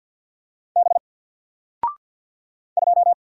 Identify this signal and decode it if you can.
Morse code, heard as a keyed tone, about 50 wpm, 700 Hz. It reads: B 2
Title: Single Letter-Number Alt 50wpm